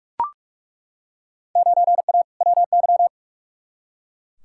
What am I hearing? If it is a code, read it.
Morse code, heard as a keyed tone, about 43 wpm, 700 Hz. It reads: 9A WY
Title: Field Day Exchanges Encore - 45wpm